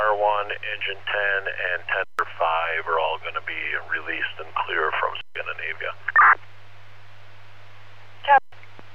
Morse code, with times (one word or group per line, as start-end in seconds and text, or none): none